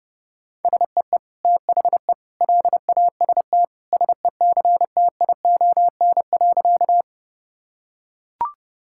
Morse code, heard as a keyed tone, about 30 wpm, 700 Hz. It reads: SEE THE LAST SECTION.
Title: Sentences from Top 1000 Words - Review Part 5 - 30wpm